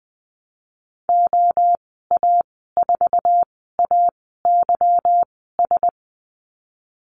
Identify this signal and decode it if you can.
Morse code, heard as a keyed tone, about 20 wpm, spaced as for 15 wpm, 700 Hz. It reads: OA4AYS
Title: 161 Lesson 26g - Letter g - Call Signs 20wpm